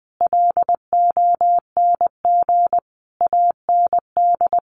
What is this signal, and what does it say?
Morse code, heard as a keyed tone, about 19 wpm, 700 Hz. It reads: LONG AND
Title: Sentences from Top 1200 Words 20wpm